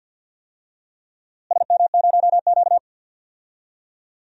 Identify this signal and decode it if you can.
Morse code, heard as a keyed tone, about 49 wpm, 700 Hz. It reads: SM0Q
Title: DXCC-100 Call Signs - 4 Characters - Repeated 3x - 50wpm